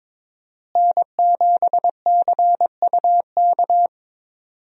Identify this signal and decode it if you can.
Morse code, heard as a keyed tone, about 22 wpm, 700 Hz. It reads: N7CUK
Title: USA Call Signs - 5 Characters - Repeated 2x - 22wpm